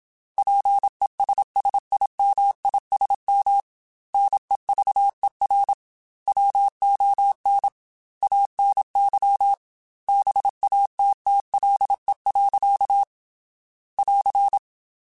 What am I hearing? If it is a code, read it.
Morse code, heard as a keyed tone, about 26 wpm, 800 Hz. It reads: PESSIMISM NEVER WON ANY BATTLE. <AR>
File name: Quotes / Quotes-mp3s / Pessimism @25